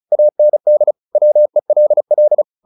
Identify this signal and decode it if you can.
Morse code, heard as a keyed tone, about 34 wpm, 600 Hz. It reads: AND WELL
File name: the old man and the sea 2 35wpm